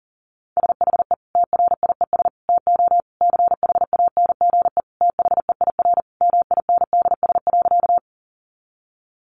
Morse code, heard as a keyed tone, about 37 wpm, 700 Hz. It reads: SHE TRIES TO CHANGE THEIR MINDS.
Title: Easy Phrases 40wpm